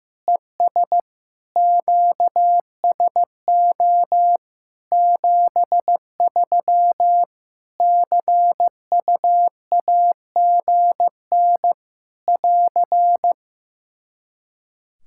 Morse code, heard as a keyed tone, about 15 wpm, 700 Hz. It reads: ES QSO 73 CUAGN <AR>
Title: K1USN SST QSO Practice 15wpm